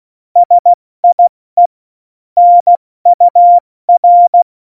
Morse code, heard as a keyed tone, about 16 wpm, spaced as for 14 wpm, 700 Hz. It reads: SIE NUR
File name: Morsekurs / Witze / Witz 02. Januar 2018, 70 F80